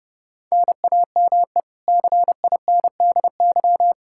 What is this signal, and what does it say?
Morse code, heard as a keyed tone, about 30 wpm, 700 Hz. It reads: NAME CINDY
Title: K1USN SST QSO Practice 30wpm